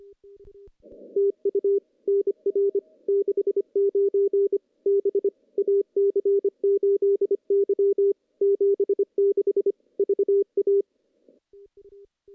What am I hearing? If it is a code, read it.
Morse code, heard as a keyed tone, about 25 wpm, 400 Hz. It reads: TU NR 69 B AC8Y 76 VA
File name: contests / CW-SS-k5tr- -0100